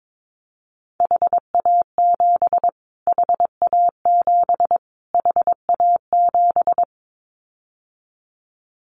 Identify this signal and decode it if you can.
Morse code, heard as a keyed tone, about 22 wpm, 700 Hz. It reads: HA7 HA7 HA7
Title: Callsign Prefixes for Top 25 DXCC - Repeated 3x - 22wpm